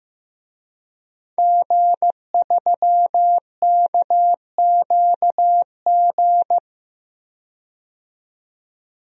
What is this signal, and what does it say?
Morse code, heard as a keyed tone, about 15 wpm, 700 Hz. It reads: G3KQG
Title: DXCC-100 Call Signs - 5 Characters - 15wpm